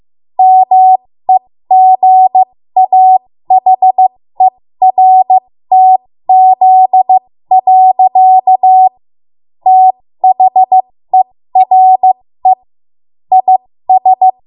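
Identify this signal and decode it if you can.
Morse code, heard as a keyed tone, about 15 wpm, spaced as for 13 wpm, 750 Hz. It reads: MEGAHERTZ. THERE IS
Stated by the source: 13 wpm Morse code practice archive transmission.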